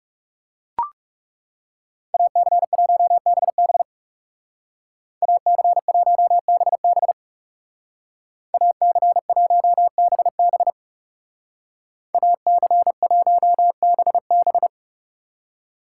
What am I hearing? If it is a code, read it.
Morse code, heard as a keyed tone, about 37 wpm, 700 Hz. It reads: AC1BB AC1BB AC1BB AC1BB
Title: Speed Racing - USA 2x2 Call Signs 30wpm